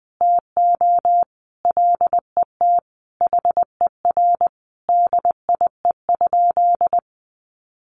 Morse code, heard as a keyed tone, about 20 wpm, 700 Hz. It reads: TO LET HER DIE?
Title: Sentences from Top 1300 Words 20wpm